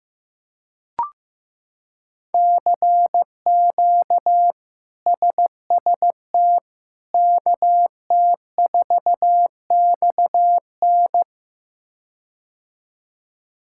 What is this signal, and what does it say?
Morse code, heard as a keyed tone, about 15 wpm, 700 Hz. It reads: CQ SST KT4XN
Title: K1USN SST QSO Practice - 15wpm